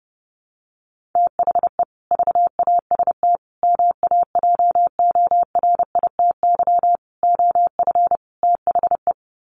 Morse code, heard as a keyed tone, about 30 wpm, 700 Hz. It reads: THE VAST MAJORITY OF THE
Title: Sets of 5 Words Encore 30wpm